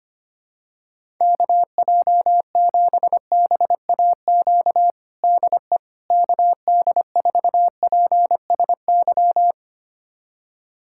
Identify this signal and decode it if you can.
Morse code, heard as a keyed tone, about 25 wpm, 700 Hz. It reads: KJ7BAQ DE KD4PSY
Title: USA Callsign DE Callsign 25wpm